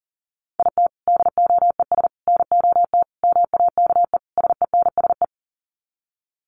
Morse code, heard as a keyed tone, about 40 wpm, 700 Hz. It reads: IT DOES NOT MAKE SENSE
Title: Sets of 5 Words Encore2 40wpm